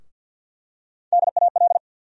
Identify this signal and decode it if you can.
Morse code, heard as a keyed tone, about 53 wpm, 700 Hz. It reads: DAL
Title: USA Call Sign Postfixes - 3 Letters 50wpm